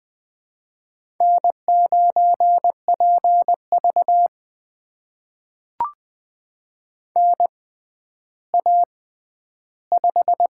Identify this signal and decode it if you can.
Morse code, heard as a keyed tone, about 20 wpm, spaced as for 11 wpm, 700 Hz. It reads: N9PV NA5
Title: USA Call Signs - 4 Characters - 6x Character Spacing - 20wpm